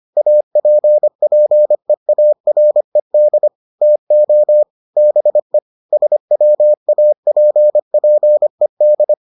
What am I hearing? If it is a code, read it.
Morse code, heard as a keyed tone, about 25 wpm, 600 Hz. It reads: APPEARED TO BE SWAPPED